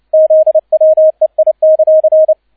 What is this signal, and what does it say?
Morse code, heard as a keyed tone, about 29 wpm, 630 Hz, with radio noise. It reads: ZWEI;
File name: 002 Zweiter Streich 30wpm